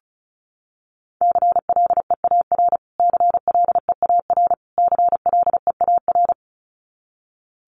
Morse code, heard as a keyed tone, about 34 wpm, 700 Hz. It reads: CLEAR CLEAR CLEAR
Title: Top 500 Words Repeated 35wpm